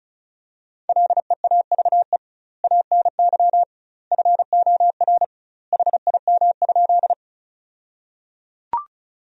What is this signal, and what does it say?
Morse code, heard as a keyed tone, about 36 wpm, 700 Hz. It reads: LEAVE ANY FOR HIM?
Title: 2x Word-Spacing - Sentences from Top 400 Words - 35wpm